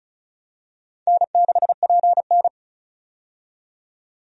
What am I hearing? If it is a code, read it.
Morse code, heard as a keyed tone, about 36 wpm, 700 Hz. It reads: N6PN
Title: USA Call Signs - 4 Characters - 35wpm